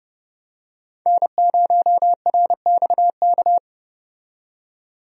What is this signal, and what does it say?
Morse code, heard as a keyed tone, about 29 wpm, 700 Hz. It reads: N0RXK
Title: USA Call Signs - 5 Characters - 30wpm